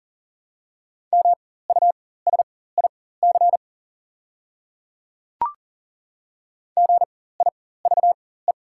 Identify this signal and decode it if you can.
Morse code, heard as a keyed tone, about 40 wpm, spaced as for 19 wpm, 700 Hz. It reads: MUSIC GIVE
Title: 4x Character Spacing - Top 300 Words - 40wpm